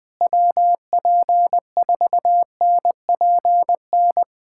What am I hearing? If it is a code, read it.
Morse code, heard as a keyed tone, about 19 wpm, 700 Hz. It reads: WP4NPN
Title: USA Callsign DE Callsign 20wpm